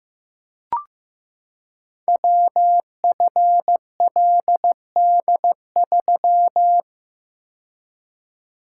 Morse code, heard as a keyed tone, about 15 wpm, 700 Hz. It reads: WFLD3
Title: Code Groups - 5 Letter:Numbers - 15wpm